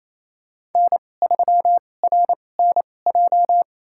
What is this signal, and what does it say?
Morse code, heard as a keyed tone, about 28 wpm, spaced as for 20 wpm, 700 Hz. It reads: N3RNJ
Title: USA Call Signs - 5 Characters - 2x Character Spacing - 28wpm